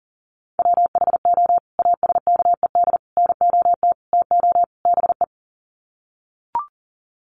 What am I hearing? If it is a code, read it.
Morse code, heard as a keyed tone, about 40 wpm, 700 Hz. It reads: WHO ASKED NOT TO BE
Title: Sets of 5 Words Encore2 40wpm